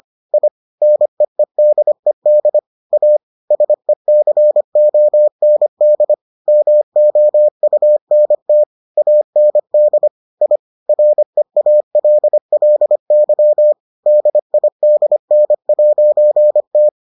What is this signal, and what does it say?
Morse code, heard as a keyed tone, about 25 wpm, 600 Hz. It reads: I NEEDED A SECOND MOUNT AND I REALLY DIDN'T